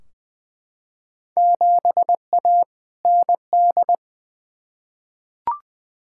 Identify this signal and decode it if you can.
Morse code, heard as a keyed tone, about 20 wpm, 700 Hz. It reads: 7A ND
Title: Field Day Exchanges Encore - 20wpm